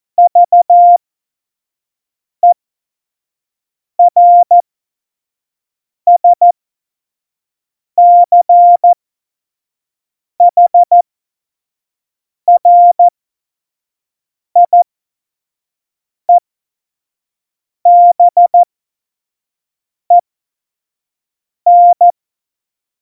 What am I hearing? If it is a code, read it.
Morse code, heard as a keyed tone, about 14 wpm, spaced as for 5 wpm, 700 Hz. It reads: VERSCHRIEBEN